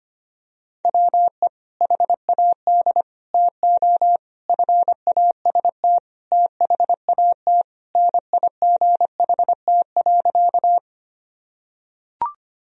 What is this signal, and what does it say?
Morse code, heard as a keyed tone, about 25 wpm, 700 Hz. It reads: WE HAD TO FAST THAT NIGHT.
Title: Sentences from Top 1000 Words - Review Part 2 - 25wpm